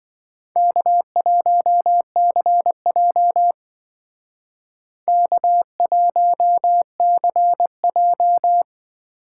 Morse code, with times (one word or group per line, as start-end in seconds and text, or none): 0.6-3.5 K1CJ
5.1-8.6 K1CJ